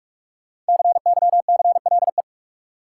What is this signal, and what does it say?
Morse code, heard as a keyed tone, about 46 wpm, 700 Hz. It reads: CYCLE
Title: Five Letter Words - 45wpm